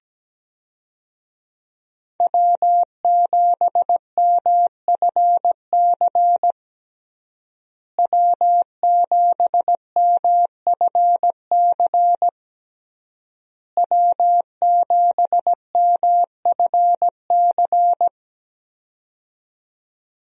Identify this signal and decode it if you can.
Morse code, heard as a keyed tone, about 17 wpm, 700 Hz. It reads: W7MFC W7MFC W7MFC
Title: USA Call Signs - 5 Characters - Repeated 3x - 17wpm